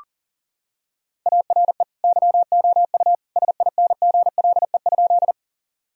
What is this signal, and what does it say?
Morse code, heard as a keyed tone, about 40 wpm, 700 Hz. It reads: ARE YOU SINGLE?
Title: Sentences from Top 1000 Words - Review Part 4 - 40wpm